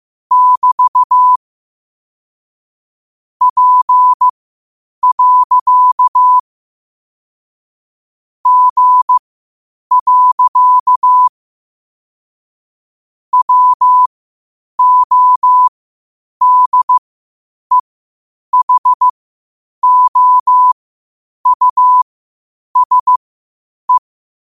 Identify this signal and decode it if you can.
Morse code, heard as a keyed tone, about 15 wpm, spaced as for 8 wpm, 1000 Hz. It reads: <BT> P. G. WODEHOUSE
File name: qotdfiles / 8wpm